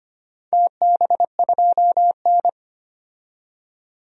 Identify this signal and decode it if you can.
Morse code, heard as a keyed tone, about 25 wpm, 700 Hz. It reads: TB2N
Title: DXCC-100 Call Signs - 4 Characters - 25wpm